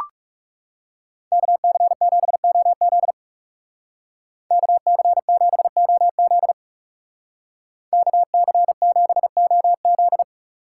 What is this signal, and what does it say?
Morse code, heard as a keyed tone, about 35 wpm, 700 Hz. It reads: KC7OZ KC7OZ KC7OZ
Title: Speed Racing - USA 2x2 Call Signs 30wpm